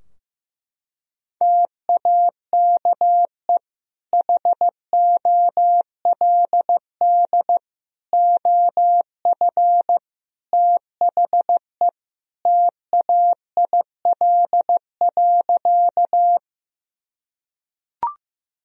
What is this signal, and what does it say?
Morse code, heard as a keyed tone, about 15 wpm, 700 Hz. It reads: TAKE HOLD OF THE TAIL.
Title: Sentences from Top 1000 Words - Review Part 3 - 15wpm